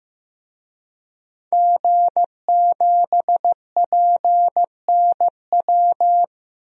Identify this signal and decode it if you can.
Morse code, heard as a keyed tone, about 15 wpm, 700 Hz. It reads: G7PNW
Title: DXCC-100 Call Signs - 5 Characters - 15wpm